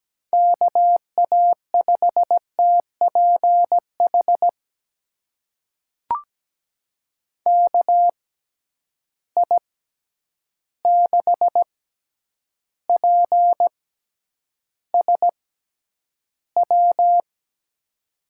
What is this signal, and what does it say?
Morse code, heard as a keyed tone, about 17 wpm, spaced as for 9 wpm, 700 Hz. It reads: KA5TPH KI6PSW
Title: USA Call Signs - 6 Characters - 6x Character Spacing - 17wpm